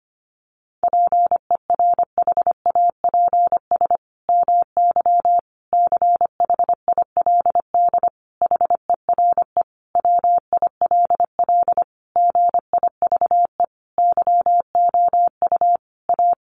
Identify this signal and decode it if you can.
Morse code, heard as a keyed tone, about 25 wpm, 700 Hz. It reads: PERHAPS MY CHILD HERE WILL GIVE YOU A
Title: Sentences from Top 1100 Words 25wpm